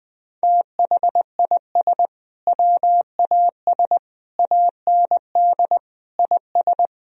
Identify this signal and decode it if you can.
Morse code, heard as a keyed tone, about 20 wpm, 700 Hz. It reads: THIS WAS AND IS
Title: Sentences from Top 1000 Words - Review Part 4 - 20wpm